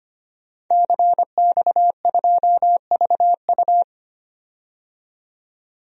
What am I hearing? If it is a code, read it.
Morse code, heard as a keyed tone, about 25 wpm, 700 Hz. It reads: CX2VU
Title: DXCC-100 Call Signs - 5 Characters - 25wpm